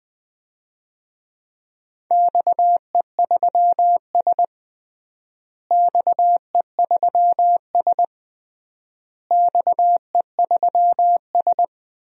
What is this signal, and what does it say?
Morse code, heard as a keyed tone, about 21 wpm, 700 Hz. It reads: XE3S XE3S XE3S
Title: DXCC-100 Call Signs - 4 Characters - Repeated 3x - 20wpm